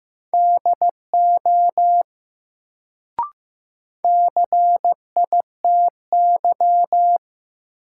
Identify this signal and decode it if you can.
Morse code, heard as a keyed tone, about 15 wpm, 700 Hz. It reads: DO CITY
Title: Sets of 2 Words - New Word Drill - 15wpm